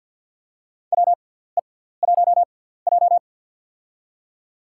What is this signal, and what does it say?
Morse code, heard as a keyed tone, about 51 wpm, spaced as for 17 wpm, 700 Hz. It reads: WE1J
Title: USA Call Signs - 4 Characters - 6x Character Spacing - 50wpm